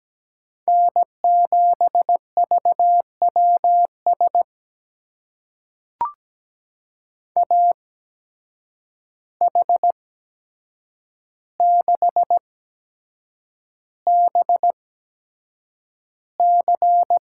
Morse code, heard as a keyed tone, about 17 wpm, spaced as for 7 wpm, 700 Hz. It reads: N7VWS AH6BC
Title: USA Call Signs - 5 Characters - 8x Character Spacing - 17wpm